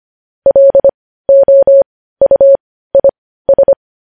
Morse code, heard as a keyed tone, about 25 wpm, spaced as for 15 wpm, 550 Hz. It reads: LOUIS
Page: News Headlines in Morse Code at 15 WPM